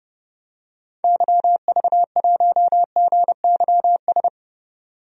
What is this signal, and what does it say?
Morse code, heard as a keyed tone, about 31 wpm, 690 Hz. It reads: YV1GYS
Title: DXCC-100 Call Signs - 6 Characters - 30wpm